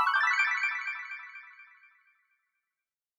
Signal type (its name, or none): none